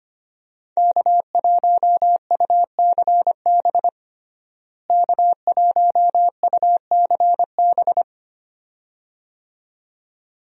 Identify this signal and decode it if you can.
Morse code, heard as a keyed tone, about 25 wpm, 700 Hz. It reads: K1UCB K1UCB
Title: USA Call Signs - 5 Characters - Repeated 2x - 25wpm